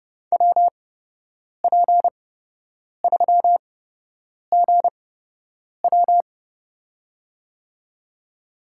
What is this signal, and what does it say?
Morse code, heard as a keyed tone, about 31 wpm, spaced as for 8 wpm, 700 Hz. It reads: WP3GW
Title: USA Call Signs - 5 Characters - 8x Character Spacing - 30wpm